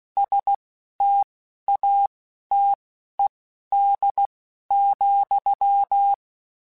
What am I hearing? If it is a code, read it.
Morse code, heard as a keyed tone, about 16 wpm, spaced as for 11 wpm, 800 Hz. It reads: STATED,
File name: Technical Subjects / Technical Subjects-mp3s / XC @10